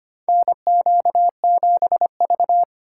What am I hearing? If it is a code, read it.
Morse code, heard as a keyed tone, about 25 wpm, 700 Hz. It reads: NQ7V
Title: USA Call Signs - 8x Character Spacing - 25wpm